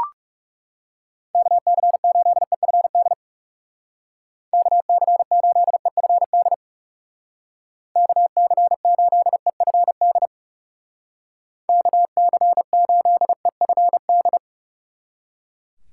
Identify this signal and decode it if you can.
Morse code, heard as a keyed tone, about 36 wpm, 700 Hz. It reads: KC8EFD KC8EFD KC8EFD KC8EFD
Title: Speed Racing - USA Call Signs 30wpm